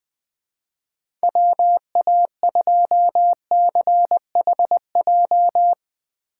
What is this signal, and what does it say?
Morse code, heard as a keyed tone, about 20 wpm, 700 Hz. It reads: WA2CHJ
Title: USA Call Signs - 6 Characters - 4x Character Spacing - 20wpm